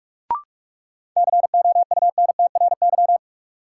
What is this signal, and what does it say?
Morse code, heard as a keyed tone, about 42 wpm, 700 Hz. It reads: COUNTRY
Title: Sentences from Top 300 Words - New Word Drill - 45wpm